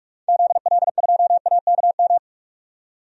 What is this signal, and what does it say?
Morse code, heard as a keyed tone, about 42 wpm, 700 Hz. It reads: ZL2AKM
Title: DXCC-100 Call Signs - 6 Characters - 45wpm